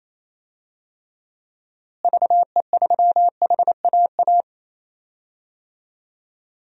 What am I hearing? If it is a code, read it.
Morse code, heard as a keyed tone, about 28 wpm, 700 Hz. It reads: VE3HAA
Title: DXCC-100 Call Signs - 6 Characters - Repeated 2x - 28wpm